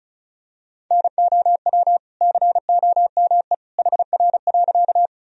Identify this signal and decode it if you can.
Morse code, heard as a keyed tone, about 34 wpm, 690 Hz. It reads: NOW COME HR.
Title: Sentences from Top 200 Words and Abbreviations 35wpm